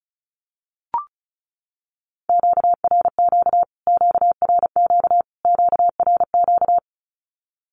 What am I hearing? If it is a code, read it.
Morse code, heard as a keyed tone, about 34 wpm, 700 Hz. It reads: QRQ QRQ QRQ
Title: Q-Codes Repeated 3x 35wpm